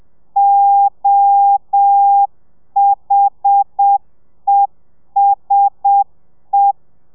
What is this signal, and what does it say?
Morse code, heard as a keyed tone, about 7 wpm, 800 Hz. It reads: OHESE
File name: telegrafia / i2tfj / Lez1 30